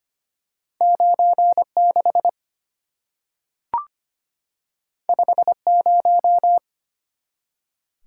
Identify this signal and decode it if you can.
Morse code, heard as a keyed tone, about 25 wpm, 700 Hz. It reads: 96 50
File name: Two Letter-Number Combinations - 25wpm